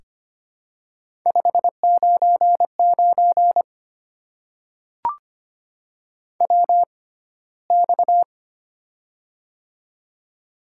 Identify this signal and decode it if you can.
Morse code, heard as a keyed tone, about 25 wpm, spaced as for 15 wpm, 700 Hz. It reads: 599 WX
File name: Top 90 QSO Elements - 6x Character Spacing - 25wpm